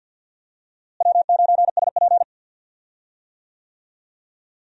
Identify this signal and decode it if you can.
Morse code, heard as a keyed tone, about 51 wpm, 700 Hz. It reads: W9SP
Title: USA Call Signs - 4 Characters - Repeated 3x - 50wpm